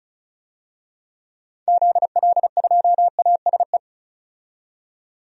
Morse code, heard as a keyed tone, about 36 wpm, 700 Hz. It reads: ZL2ASE
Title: DXCC-100 Call Signs - 6 Characters - 35wpm